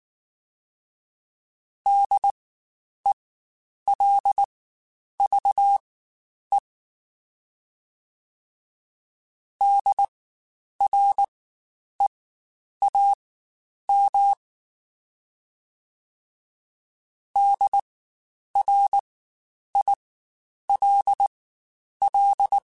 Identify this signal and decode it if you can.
Morse code, heard as a keyed tone, about 19 wpm, spaced as for 9 wpm, 800 Hz. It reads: DELVE DREAM DRILL
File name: Spaced Alphas / Spaced Alphas mp3s / Ds sp @18